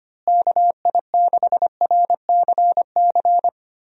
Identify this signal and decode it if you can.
Morse code, heard as a keyed tone, about 25 wpm, 700 Hz. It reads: KI6RCC
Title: USA Call Signs - 6 Characters - Repeated 2x - 25wpm